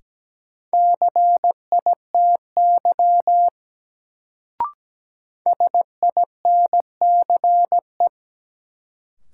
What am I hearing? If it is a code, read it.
Morse code, heard as a keyed tone, about 17 wpm, 700 Hz. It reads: CITY SINCE
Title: Sentences from Top 300 Words - New Word Drill - 17wpm